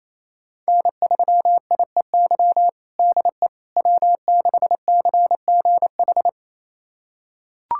Morse code, heard as a keyed tone, about 28 wpm, 700 Hz. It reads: N3IEY DE W6CGH
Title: USA Callsign DE Callsign - 5 Characters - 28wpm